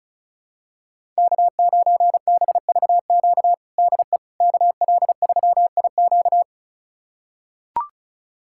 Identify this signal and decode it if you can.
Morse code, heard as a keyed tone, about 36 wpm, 700 Hz. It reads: K9BVQ DE KL3IQ
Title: USA Callsign DE Callsign - 5 Characters - 35wpm